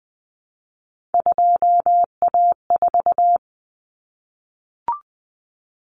Morse code, heard as a keyed tone, about 20 wpm, 700 Hz. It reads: 2A4
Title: Common Cut Numbers - 3-Digit 20wpm